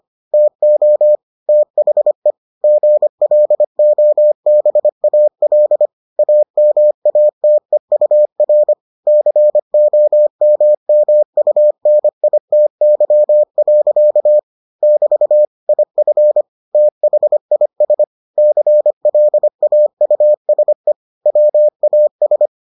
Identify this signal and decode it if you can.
Morse code, heard as a keyed tone, about 25 wpm, 600 Hz. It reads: TO THE GLOBAL AMATEUR COMMUNITY. <BT> IF THIS CLAUSE WAS